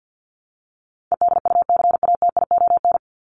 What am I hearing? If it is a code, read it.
Morse code, heard as a keyed tone, about 48 wpm, 700 Hz. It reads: EDUCATION
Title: Top 751-1000 Words 50wpm